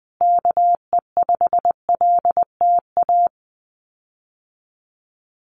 Morse code, heard as a keyed tone, about 20 wpm, 700 Hz. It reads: KE5LTA
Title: USA Call Signs Repeated 3x 20wpm